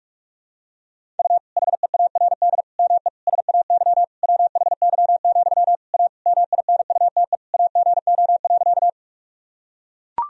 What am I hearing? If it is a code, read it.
Morse code, heard as a keyed tone, about 42 wpm, 690 Hz. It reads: U HEARD ME SAY WHY, A MINUTE AGO.
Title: Sentences from Top 500 Words and Abbreviations 45wpm